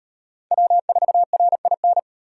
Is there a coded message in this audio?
Yes: Morse code from 0.5 to 2.0 s, heard as a keyed tone, about 38 wpm, 700 Hz. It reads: W4RIN